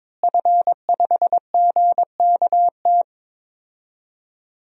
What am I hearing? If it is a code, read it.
Morse code, heard as a keyed tone, about 21 wpm, 700 Hz. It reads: F5GKT
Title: DXCC-100 Call Signs - 5 Characters - Repeated 3x - 22wpm